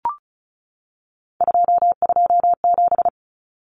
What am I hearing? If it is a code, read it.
Morse code, heard as a keyed tone, about 34 wpm, 700 Hz. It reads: 227
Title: Common Cut Numbers - 3-Digit 35wpm